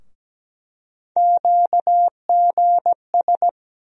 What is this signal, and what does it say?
Morse code, heard as a keyed tone, about 17 wpm, 700 Hz. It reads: QGS
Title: USA Call Sign Postfixes - 3 Letters 17wpm